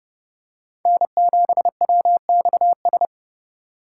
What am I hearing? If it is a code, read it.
Morse code, heard as a keyed tone, about 28 wpm, 700 Hz. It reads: N7WXS